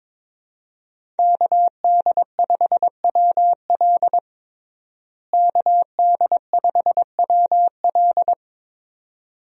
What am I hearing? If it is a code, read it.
Morse code, heard as a keyed tone, about 21 wpm, 700 Hz. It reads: KD5WL KD5WL
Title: USA Call Signs - 5 Characters - Repeated 2x - 22wpm